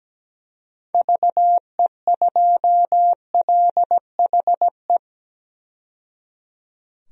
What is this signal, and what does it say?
Morse code, heard as a keyed tone, about 17 wpm, 700 Hz. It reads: VE2LHE
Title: DXCC-100 Call Signs - 6 Characters - 17wpm